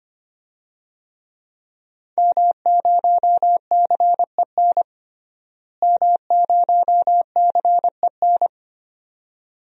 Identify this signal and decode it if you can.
Morse code, heard as a keyed tone, about 25 wpm, 700 Hz. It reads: M0CEN M0CEN
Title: DXCC-100 Call Signs - 5 Characters - Repeated 2x - 25wpm